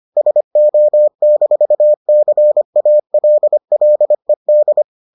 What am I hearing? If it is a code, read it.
Morse code, heard as a keyed tone, about 25 wpm, 600 Hz. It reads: SO-CALLED